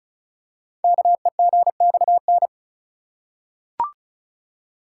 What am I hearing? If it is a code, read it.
Morse code, heard as a keyed tone, about 36 wpm, 700 Hz. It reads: KEGXN
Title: Code Groups - 5 Letters - 35wpm